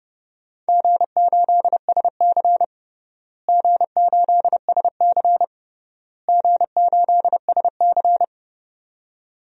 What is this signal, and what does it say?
Morse code, heard as a keyed tone, about 30 wpm, 700 Hz. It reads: G8SC G8SC G8SC
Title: DXCC-100 Call Signs - 4 Characters - Repeated 3x - 30wpm